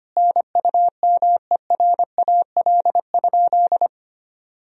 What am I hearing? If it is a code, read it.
Morse code, heard as a keyed tone, about 25 wpm, 700 Hz. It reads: NUMERAL?
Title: Sentences from Top 1000 Words - Review Part 2 - 25wpm